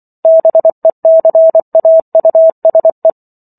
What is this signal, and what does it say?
Morse code, heard as a keyed tone, about 24 wpm, 650 Hz. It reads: BECAUSE